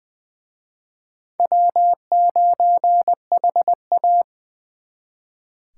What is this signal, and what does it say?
Morse code, heard as a keyed tone, about 24 wpm, 700 Hz. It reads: W9HA
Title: Speed Racing - USA 1x2 Call Signs 20wpm